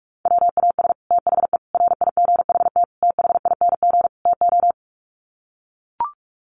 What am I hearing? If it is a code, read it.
Morse code, heard as a keyed tone, about 44 wpm, 700 Hz. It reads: WAS THE RIGHT THING TO
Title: Sets of 5 Words Encore2 45wpm